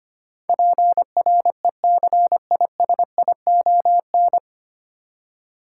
Morse code, heard as a keyed tone, about 25 wpm, 700 Hz. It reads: PRECISION
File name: practice / 25wpm / 9 Letter Words - First 3 Letters Given - 25wpm